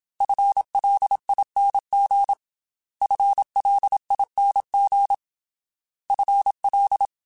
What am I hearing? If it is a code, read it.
Morse code, heard as a keyed tone, about 27 wpm, 800 Hz. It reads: FLING FLING FL
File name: FL / FL mp3s / FL Prefixes 3x2 Twice @25